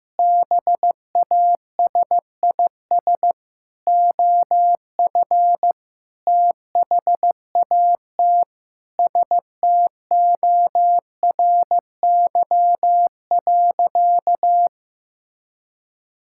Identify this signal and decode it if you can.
Morse code, heard as a keyed tone, about 16 wpm, 700 Hz. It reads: BASIS OF THAT STORY.
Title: Sentences from Top 1000 Words - Review Part 5 - 15wpm